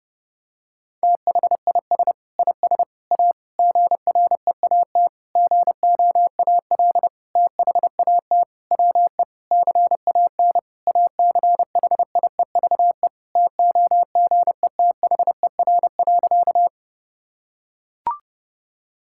Morse code, heard as a keyed tone, about 32 wpm, 700 Hz. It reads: THIS IS A GREAT GOAL THAT WE CAN ACHIEVE TOGETHER.
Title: Sentences from Top 1000 Words - Review Part 6 - 30wpm